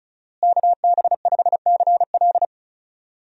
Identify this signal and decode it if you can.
Morse code, heard as a keyed tone, about 36 wpm, 700 Hz. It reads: KB5CL
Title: USA Call Signs - 5 Characters - Repeated 3x - 35wpm